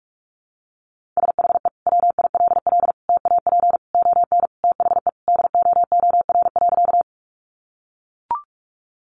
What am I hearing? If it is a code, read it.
Morse code, heard as a keyed tone, about 44 wpm, 700 Hz. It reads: SHE WILL TAP ON THE DOOR.
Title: Sentences from Top 1400 Words 45wpm